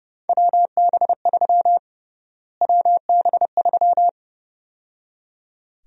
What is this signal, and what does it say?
Morse code, heard as a keyed tone, about 32 wpm, 700 Hz. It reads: WB3 WB3
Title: USA Call Sign Prefixes - Repeated 2x - 30wpm